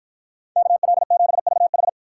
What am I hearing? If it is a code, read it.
Morse code, heard as a keyed tone, about 53 wpm, 700 Hz. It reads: KL7VH